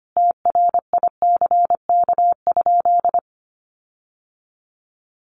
Morse code, heard as a keyed tone, about 25 wpm, 700 Hz. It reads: TRICK?
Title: Sentences from Top 1500 Words 25wpm